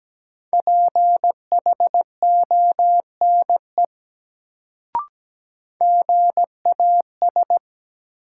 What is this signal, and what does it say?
Morse code, heard as a keyed tone, about 17 wpm, 700 Hz. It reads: PHONE GAS
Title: Sentences from Top 1100 Words - New Word Drill - 17wpm